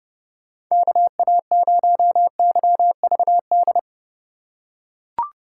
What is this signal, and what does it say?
Morse code, heard as a keyed tone, about 32 wpm, 700 Hz. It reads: KA0YVD
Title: USA Call Signs - 6 Characters - 4x Character Spacing - 30wpm